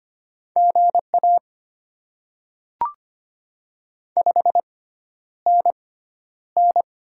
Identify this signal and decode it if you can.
Morse code, heard as a keyed tone, about 25 wpm, spaced as for 11 wpm, 700 Hz. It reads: GA 5NN
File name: Top 20 QSO Elements - 6x Character Spacing - 25wpm